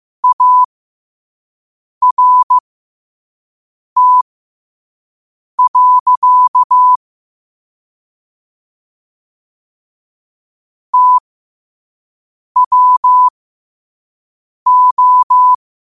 Morse code, heard as a keyed tone, about 15 wpm, spaced as for 5 wpm, 1000 Hz. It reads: ART. TWO
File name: qotdfiles / 5wpm